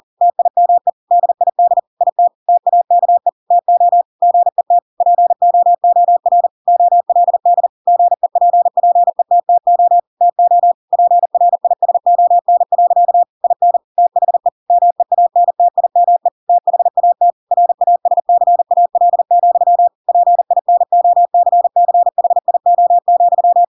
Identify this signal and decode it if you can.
Morse code, heard as a keyed tone, about 40 wpm, 700 Hz. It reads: TIME DID IT TAKE TO GET POOR OLD GEPPETTO TO PRISON. IN THE MEANTIME THAT RASCAL, PINOCCHIO,
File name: Pinocchio / Pinocchio ch4 40 40 ik1hge